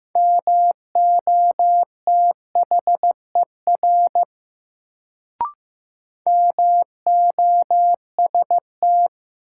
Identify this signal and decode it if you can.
Morse code, heard as a keyed tone, about 16 wpm, 700 Hz. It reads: MOTHER MOST
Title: Sentences from Top 200 Words - New Word Drill - 15wpm